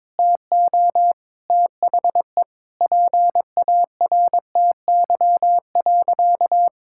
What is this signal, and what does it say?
Morse code, heard as a keyed tone, about 22 wpm, 700 Hz. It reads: TO THE PARTY.
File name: practice / 22wpm / Sentences with 3-Word Phrases - 22wpm